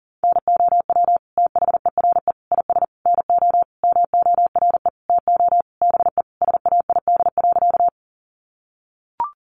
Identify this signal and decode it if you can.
Morse code, heard as a keyed tone, about 40 wpm, 700 Hz. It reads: NOW THERE IS NO MORE TO BE SAID.
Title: Sentences from Top 100 Words 40wpm